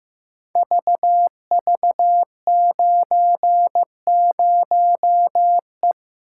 Morse code, heard as a keyed tone, about 15 wpm, 700 Hz. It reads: VV90E
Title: Code Groups - 5 Letter:Numbers - 15wpm